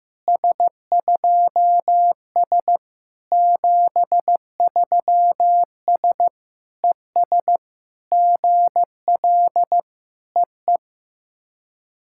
Morse code, heard as a keyed tone, about 14 wpm, 700 Hz. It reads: S2S 73S ES GL EE
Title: SOTA QSO Practice 15wpm